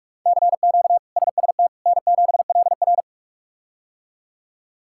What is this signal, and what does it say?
Morse code, heard as a keyed tone, about 45 wpm, 700 Hz. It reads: CQ SST N7LR
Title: K1USN SST QSO Practice - 45wpm